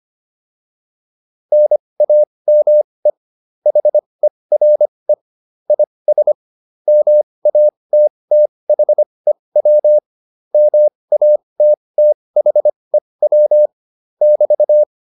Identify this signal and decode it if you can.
Morse code, heard as a keyed tone, about 25 wpm, spaced as for 20 wpm, 600 Hz. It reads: NAME HERE IS MATTHEW MATTHEW <BT>